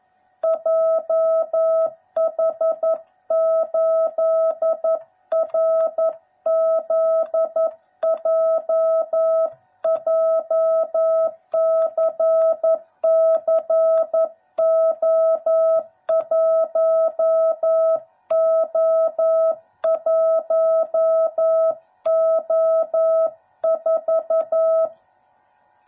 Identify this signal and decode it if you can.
Morse code, heard as a keyed tone, about 11 wpm, 660 Hz. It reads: JH8RZJ JCCO1O1O4
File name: jh8cbh / 2021cw / cw09b